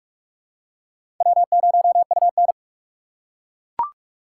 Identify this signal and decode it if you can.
Morse code, heard as a keyed tone, about 43 wpm, 700 Hz. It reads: W0UN
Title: USA Call Signs - 4 Characters - 8x Character Spacing - 45wpm